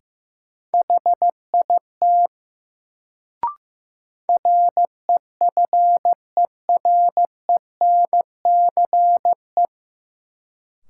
Morse code, heard as a keyed tone, about 16 wpm, 700 Hz. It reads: HIT REFERENCE
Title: Sentences from Top 1200 Words - New Word Drill Alt - 15wpm